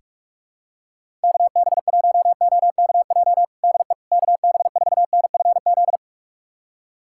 Morse code, heard as a keyed tone, about 46 wpm, 700 Hz. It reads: KB1OKJ DE KB4NFZ
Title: USA Callsign DE Callsign 45wpm